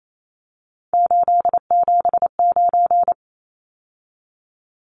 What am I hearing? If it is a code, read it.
Morse code, heard as a keyed tone, about 28 wpm, 700 Hz. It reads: 879
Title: Numbers - 3-Digit 28wpm